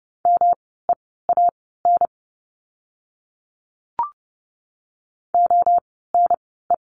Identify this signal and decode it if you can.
Morse code, heard as a keyed tone, about 30 wpm, spaced as for 17 wpm, 700 Hz. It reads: MEAN ONE
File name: course / 30:10wpm / 237 Lesson 38f - Letter q - Words 30wpm